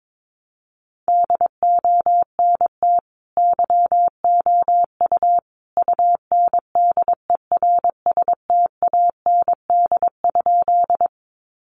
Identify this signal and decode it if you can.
Morse code, heard as a keyed tone, about 22 wpm, 700 Hz. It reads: DONT YOU UNDERSTAND?
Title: Sentences from Top 600 Words 22wpm